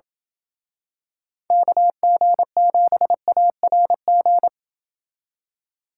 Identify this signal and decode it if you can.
Morse code, heard as a keyed tone, about 27 wpm, 700 Hz. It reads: KG7ARG